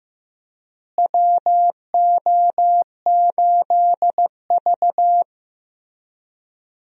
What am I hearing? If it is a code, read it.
Morse code, heard as a keyed tone, about 15 wpm, 700 Hz. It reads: WO8V
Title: USA Call Signs - 4 Characters - Repeated 2x - 15wpm